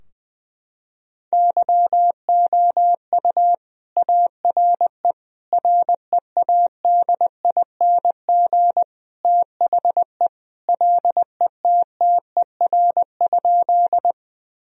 Morse code, heard as a keyed tone, about 20 wpm, 700 Hz. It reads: YOU ARE READING THE LETTER?
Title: Sentences from Top 1000 Words - Review Part 3 - 20wpm